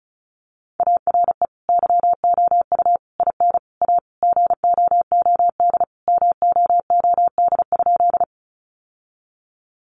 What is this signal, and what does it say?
Morse code, heard as a keyed tone, about 34 wpm, 700 Hz. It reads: ARE YOU IN A GOOD MOOD?
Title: Sentences from Top 1100 Words 35wpm